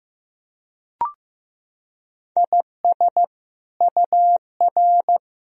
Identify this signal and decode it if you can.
Morse code, heard as a keyed tone, about 16 wpm, 700 Hz. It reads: IS UR
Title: Sentences from Top 400 Words and Abbreviations 15wpm